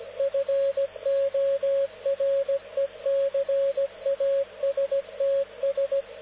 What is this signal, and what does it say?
Morse code, heard as a keyed tone, about 17 wpm, 550 Hz, with radio noise. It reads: FORECASTS
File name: luxorion / Radio / dk0wcy-spaceweather-10144khz